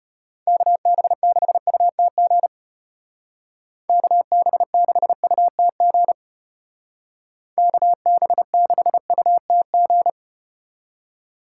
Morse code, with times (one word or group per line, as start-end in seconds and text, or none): 0.5-2.5 KB6UTG
3.9-6.1 KB6UTG
7.6-10.1 KB6UTG